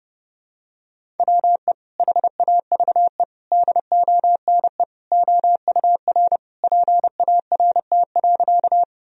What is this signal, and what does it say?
Morse code, heard as a keyed tone, about 30 wpm, 700 Hz. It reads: WE HAVE DONE OUR PART.
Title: Sentences from Top 1000 Words - Review Part 1 - 30wpm